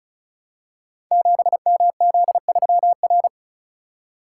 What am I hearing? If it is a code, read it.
Morse code, heard as a keyed tone, about 36 wpm, 700 Hz. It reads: 7MZ3R
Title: Code Groups - 5 Letter:Numbers - 35wpm